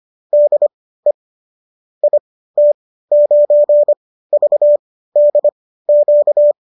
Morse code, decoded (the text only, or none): DE IT9VDQ